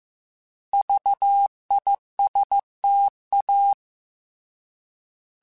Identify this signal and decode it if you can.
Morse code, heard as a keyed tone, about 15 wpm, 800 Hz. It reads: VISTA
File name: Alphabetical / Alphabetical mp3s / Vs @13